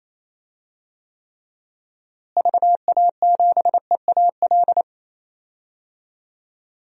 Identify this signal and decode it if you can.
Morse code, heard as a keyed tone, about 28 wpm, 700 Hz. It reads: VA7EAL
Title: DXCC-100 Call Signs - 6 Characters - Repeated 3x - 28wpm